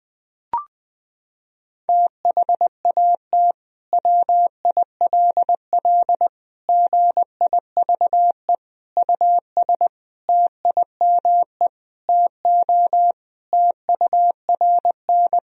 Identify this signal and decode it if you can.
Morse code, heard as a keyed tone, about 20 wpm, 700 Hz. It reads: THAT WILL GIVE US TIME TO TURN
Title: Sentences from Top 1000 Words - Review Part 1 - 20wpm